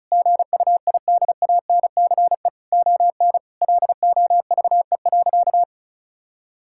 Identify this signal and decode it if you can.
Morse code, heard as a keyed tone, about 34 wpm, 700 Hz. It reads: GUIDANCE ON LOVE.
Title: Sentences from Top 1600 Words 35wpm